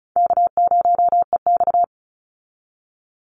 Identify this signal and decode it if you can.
Morse code, heard as a keyed tone, about 34 wpm, 700 Hz. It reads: K0EX
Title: Easy Call Signs Repeated 2x - 35wpm